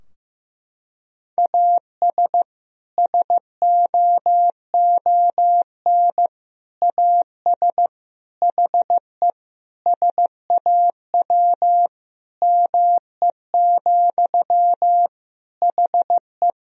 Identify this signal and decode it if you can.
Morse code, heard as a keyed tone, about 16 wpm, 700 Hz. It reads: AS SOON AS HE SAW ME, HE
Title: Sentences from Top 1000 Words - Review Part 2 - 15wpm